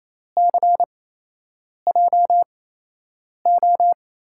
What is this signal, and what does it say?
Morse code, heard as a keyed tone, about 28 wpm, spaced as for 8 wpm, 700 Hz. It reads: CJO
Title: USA Call Signs - 6 Characters - 8x Character Spacing - 28wpm